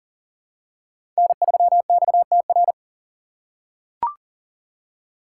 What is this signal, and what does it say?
Morse code, heard as a keyed tone, about 40 wpm, 700 Hz. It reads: N3XTR
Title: USA Call Signs - 5 Characters - 3x Character Spacing - 40wpm